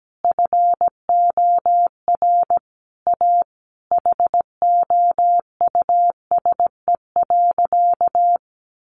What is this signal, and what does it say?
Morse code, heard as a keyed tone, about 17 wpm, 700 Hz. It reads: FOR A HOUSE.
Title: Sentences from Top 600 Words 17wpm